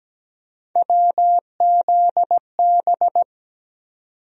Morse code, heard as a keyed tone, about 17 wpm, 700 Hz. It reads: WZB
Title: USA Call Sign Postfixes - 3 Letters 17wpm